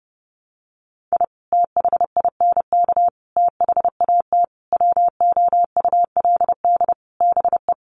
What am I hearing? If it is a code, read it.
Morse code, heard as a keyed tone, about 30 wpm, 700 Hz. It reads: I THINK THAT WOULD BE
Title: Sets of 5 Words Encore2 30wpm